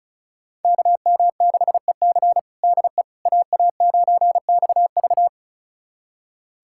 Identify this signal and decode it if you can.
Morse code, heard as a keyed tone, about 36 wpm, 700 Hz. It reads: KM6EC DE AA9XV
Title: USA Callsign DE Callsign - 5 Characters - 35wpm